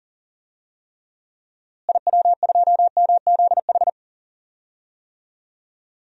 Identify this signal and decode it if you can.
Morse code, heard as a keyed tone, about 40 wpm, 700 Hz. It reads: IW2MZH
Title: DXCC-100 Call Signs - 6 Characters - Repeated 3x - 40wpm